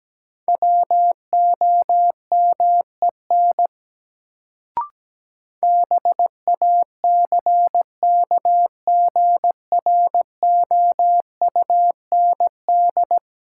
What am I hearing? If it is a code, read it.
Morse code, heard as a keyed tone, about 17 wpm, 700 Hz. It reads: WOMEN BACKGROUND
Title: Sentences from Top 1300 Words - New Word Drill - 17wpm